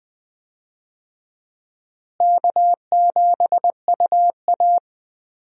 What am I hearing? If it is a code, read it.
Morse code, heard as a keyed tone, about 20 wpm, 700 Hz. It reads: K7UA
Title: USA Call Signs - 4 Characters - Repeated 2x - 20wpm